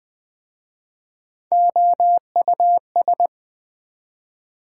Morse code, heard as a keyed tone, about 22 wpm, 700 Hz. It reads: OUS